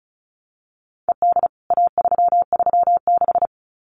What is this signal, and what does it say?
Morse code, heard as a keyed tone, about 34 wpm, 700 Hz. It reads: ED A336
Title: CWT Exchange 35wpm